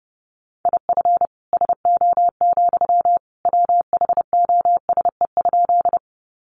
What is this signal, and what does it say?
Morse code, heard as a keyed tone, about 30 wpm, 700 Hz. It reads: IF SO, WHOSE?
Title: Sentences from Top 1200 Words 30wpm